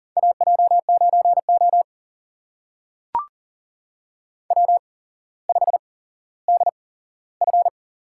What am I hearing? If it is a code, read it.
Morse code, heard as a keyed tone, about 40 wpm, spaced as for 17 wpm, 700 Hz. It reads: AJ9O W5DF
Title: USA Call Signs - 8x Character Spacing - 40wpm